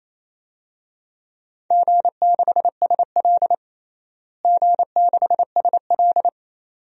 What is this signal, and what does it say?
Morse code, heard as a keyed tone, about 28 wpm, 700 Hz. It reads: G6SL G6SL
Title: DXCC-100 Call Signs - 4 Characters - Repeated 2x - 28wpm